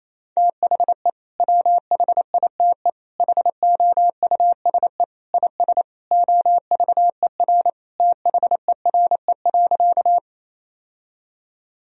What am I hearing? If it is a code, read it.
Morse code, heard as a keyed tone, about 28 wpm, 700 Hz. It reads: THE WHITE HOUSE IS OVER THERE.
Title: Sentences from Top 1000 Words - Review Part 1 - 28wpm